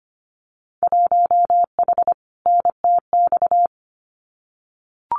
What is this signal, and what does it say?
Morse code, heard as a keyed tone, about 25 wpm, 700 Hz. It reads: 1H NTX
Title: Winter Field Day Exchanges 25wpm